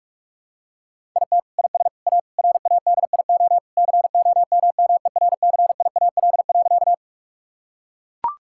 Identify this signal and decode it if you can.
Morse code, heard as a keyed tone, about 41 wpm, 700 Hz. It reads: IT IS A RADIO COMMERCIAL.
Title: Sentences from Top 1000 Words - Review Part 5 - 45wpm